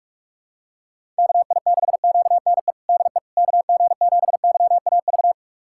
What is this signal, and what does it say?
Morse code, heard as a keyed tone, about 46 wpm, 700 Hz. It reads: KI6QNE DE KG7YAV
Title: USA Callsign DE Callsign 45wpm